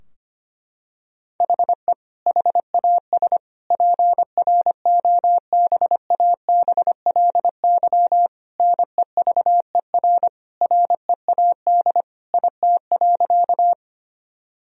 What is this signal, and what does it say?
Morse code, heard as a keyed tone, about 25 wpm, 700 Hz. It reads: HE HAS PROBABLY NEVER READ IT.
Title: Sentences from Top 1000 Words - Review Part 4 - 25wpm